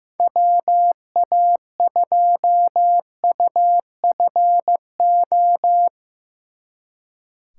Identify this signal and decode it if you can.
Morse code, heard as a keyed tone, about 17 wpm, 700 Hz. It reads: WA2UFO
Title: Speed Racing - USA 2x3 Call Signs 15wpm